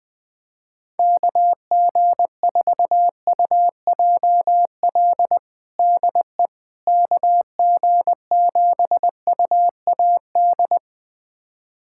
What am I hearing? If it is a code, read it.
Morse code, heard as a keyed tone, about 19 wpm, 700 Hz. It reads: KG4UJL DE KG7UAD
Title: USA Callsign DE Callsign 20wpm